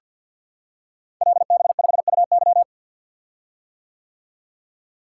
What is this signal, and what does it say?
Morse code, heard as a keyed tone, about 49 wpm, 700 Hz. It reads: LB5UY
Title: Code Groups - 5 Letter:Numbers - 50wpm